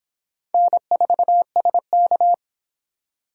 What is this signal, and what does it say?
Morse code, heard as a keyed tone, about 25 wpm, 700 Hz. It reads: N4SK